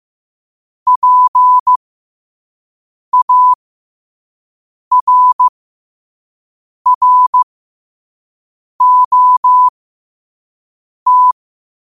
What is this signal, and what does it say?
Morse code, heard as a keyed tone, about 15 wpm, spaced as for 5 wpm, 1000 Hz. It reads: PARROT
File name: qotdfiles / 5wpm